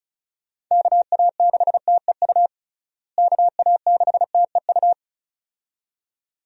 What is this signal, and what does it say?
Morse code, heard as a keyed tone, about 36 wpm, 700 Hz. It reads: KA6TEU KA6TEU
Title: USA Call Signs - 6 Characters - Repeated 2x - 35wpm